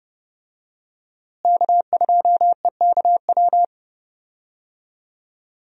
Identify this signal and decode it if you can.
Morse code, heard as a keyed tone, about 32 wpm, 700 Hz. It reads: K2EKW
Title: USA Call Signs - 5 Characters - Repeated 3x - 30wpm